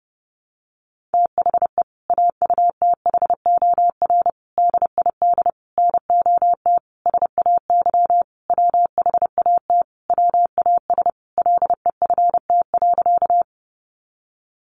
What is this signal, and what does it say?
Morse code, heard as a keyed tone, about 30 wpm, 700 Hz. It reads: THE AUTHOR DID NOT SAY WHAT WAS LEFT.
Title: Sentences from Top 1200 Words 30wpm